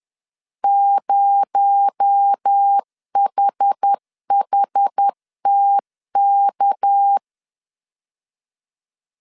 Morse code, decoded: 0HHTK

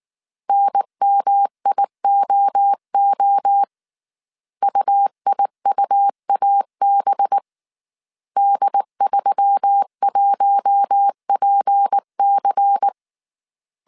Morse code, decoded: NMIOO UIUAB D31PC

